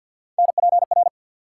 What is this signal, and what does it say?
Morse code, heard as a keyed tone, about 49 wpm, 700 Hz. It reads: NPR
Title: Sets of 5 Words - New Word Drill - 50wpm